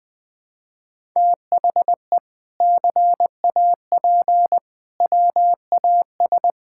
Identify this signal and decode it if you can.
Morse code, heard as a keyed tone, about 20 wpm, 700 Hz. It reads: THE CAP WAS
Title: Sentences from Top 1600 Words 20wpm